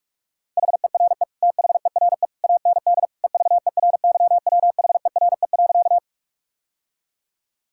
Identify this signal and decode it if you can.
Morse code, heard as a keyed tone, about 46 wpm, 700 Hz. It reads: HERE THERE AND EVERYWHERE.
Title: Sentences from Top 1600 Words 45wpm